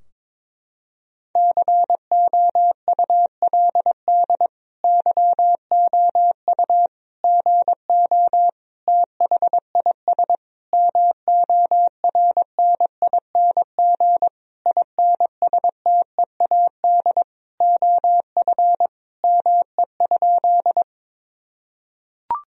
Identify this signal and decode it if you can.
Morse code, heard as a keyed tone, about 22 wpm, 700 Hz. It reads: COULD YOU GO THIS MORNING INSTEAD OF ME?
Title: Sentences from Top 1000 Words - Review Part 4 - 22wpm